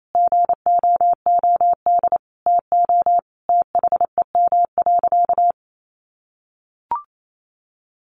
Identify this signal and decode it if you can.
Morse code, heard as a keyed tone, about 28 wpm, 700 Hz. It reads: GOOD TO THEM.
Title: Sentences from Top 1200 Words 28wpm